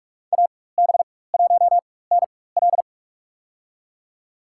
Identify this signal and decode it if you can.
Morse code, heard as a keyed tone, about 47 wpm, spaced as for 21 wpm, 690 Hz. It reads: AB1NL
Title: USA Call Signs - 4x Character Spacing - 45wpm